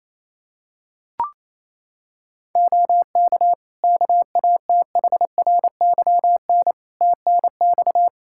Morse code, heard as a keyed tone, about 27 wpm, 700 Hz. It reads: OK KATHRYN TNX
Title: K1USN SST QSO Practice 28wpm